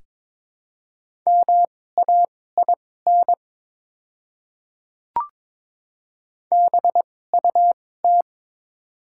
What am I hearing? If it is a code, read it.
Morse code, heard as a keyed tone, about 22 wpm, spaced as for 16 wpm, 700 Hz. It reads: MAIN BUT
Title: 2x Character Spacing - Top 300 Words - 22wpm